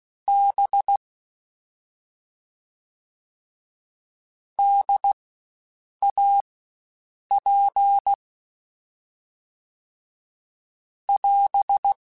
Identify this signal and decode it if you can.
Morse code, heard as a keyed tone, about 16 wpm, spaced as for 7 wpm, 800 Hz. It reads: B DAP <AS>
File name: Spaced Alphas / Spaced Alphas mp3s / Ds sp @15